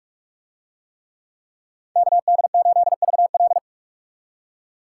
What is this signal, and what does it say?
Morse code, heard as a keyed tone, about 43 wpm, 700 Hz. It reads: KD8VL